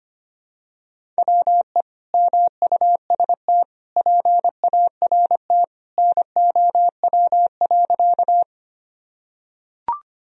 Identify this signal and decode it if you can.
Morse code, heard as a keyed tone, about 25 wpm, 700 Hz. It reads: WE MUST PART NOW.
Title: Sentences from Top 1000 Words - Review Part 1 - 25wpm